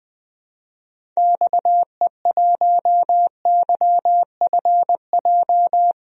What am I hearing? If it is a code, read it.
Morse code, heard as a keyed tone, about 20 wpm, 700 Hz. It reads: XE1YFJ
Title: DXCC-100 Call Signs - 6 Characters - 20wpm